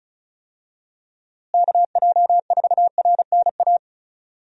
Morse code, heard as a keyed tone, about 36 wpm, 700 Hz. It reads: KJ4RNA
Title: USA Call Signs - 6 Characters - Repeated 2x - 35wpm